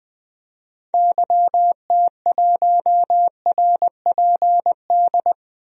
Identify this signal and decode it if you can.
Morse code, heard as a keyed tone, about 20 wpm, 700 Hz. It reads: YT1RPD
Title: DXCC-100 Call Signs - 6 Characters - 20wpm